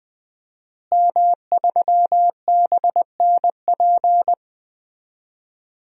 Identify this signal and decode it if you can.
Morse code, heard as a keyed tone, about 20 wpm, 700 Hz. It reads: M3BNP
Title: DXCC-100 Call Signs - 5 Characters - Repeated 3x - 20wpm